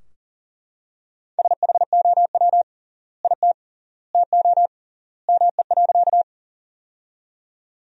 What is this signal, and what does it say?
Morse code, heard as a keyed tone, about 37 wpm, 700 Hz. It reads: SHOW IT TO ME.
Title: 3x Word-Spacing - Sentences from Top 200 Words - 40wpm